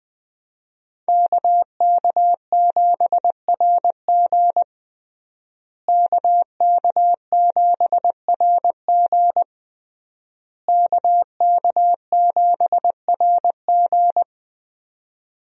Morse code, heard as a keyed tone, about 21 wpm, 700 Hz. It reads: KK7RG KK7RG KK7RG
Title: USA Call Signs - 5 Characters - Repeated 3x - 20wpm